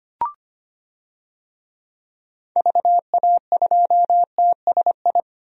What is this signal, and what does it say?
Morse code, heard as a keyed tone, about 25 wpm, 700 Hz. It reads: VA2TSI
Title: DXCC-100 Call Signs - 6 Characters - Repeated 2x - 25wpm